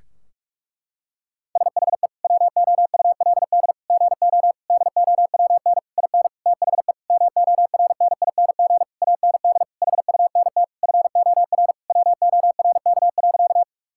Morse code, heard as a keyed tone, about 41 wpm, 700 Hz. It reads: SHE WOULD GO DOWN IN THE MORNING AND HUNT FOR WORK.
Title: Sentences from Top 1600 Words 45wpm